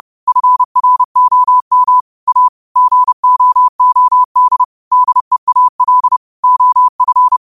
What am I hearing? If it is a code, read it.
Morse code, heard as a keyed tone, about 30 wpm, 1000 Hz. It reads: FROM A GOOD DEAL OF
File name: qotdfiles / 30wpm